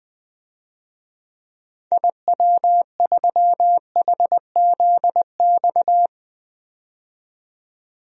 Morse code, heard as a keyed tone, about 20 wpm, 700 Hz. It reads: IW3HZX
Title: DXCC-100 Call Signs - 6 Characters - 20wpm